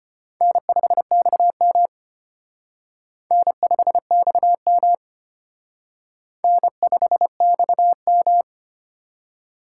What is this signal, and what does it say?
Morse code, heard as a keyed tone, about 29 wpm, 700 Hz. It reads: N5XM N5XM N5XM
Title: Speed Racing - USA 1x2 Call Signs 25wpm